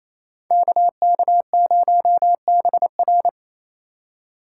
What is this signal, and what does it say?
Morse code, heard as a keyed tone, about 36 wpm, 700 Hz. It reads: KK0BR